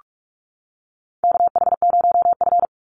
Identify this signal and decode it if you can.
Morse code, heard as a keyed tone, about 44 wpm, 700 Hz. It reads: KH0F
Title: USA Call Signs 45wpm